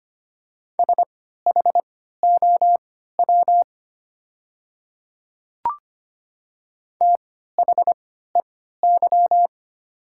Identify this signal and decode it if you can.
Morse code, heard as a keyed tone, about 25 wpm, spaced as for 14 wpm, 690 Hz. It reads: SHOW THEY
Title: 3x Character Spacing - Top 200 Words - 25wpm